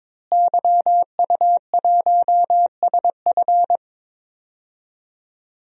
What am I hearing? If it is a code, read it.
Morse code, heard as a keyed tone, about 22 wpm, 700 Hz. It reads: YU1SF
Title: DXCC-100 Call Signs - 5 Characters - Repeated 2x - 22wpm